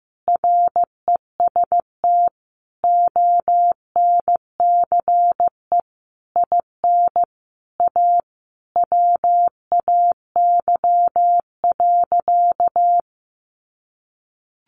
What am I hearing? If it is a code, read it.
Morse code, heard as a keyed tone, about 15 wpm, 700 Hz. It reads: REST ONCE IN A WAY.
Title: Sentences from Top 500 Words 15wpm